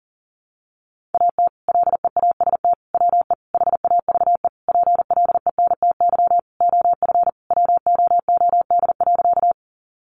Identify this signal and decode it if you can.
Morse code, heard as a keyed tone, about 40 wpm, 700 Hz. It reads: AT LEAST WE HAVE PLENTY OF WOOD.
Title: Sentences from Top 1300 Words 40wpm